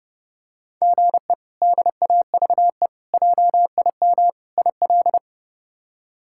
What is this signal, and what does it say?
Morse code, heard as a keyed tone, about 30 wpm, 700 Hz. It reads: GE DAVE JIM IL
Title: K1USN SST QSO Practice - 30wpm